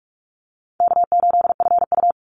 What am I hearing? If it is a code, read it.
Morse code, heard as a keyed tone, about 44 wpm, 700 Hz. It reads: K8FU
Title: Easy Call Signs Repeated 3x - 45wpm